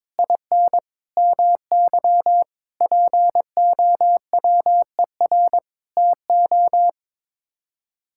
Morse code, heard as a keyed tone, about 22 wpm, 700 Hz. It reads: IN MY POWER TO
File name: practice / 22wpm / Sets of 8 Words - 22wpm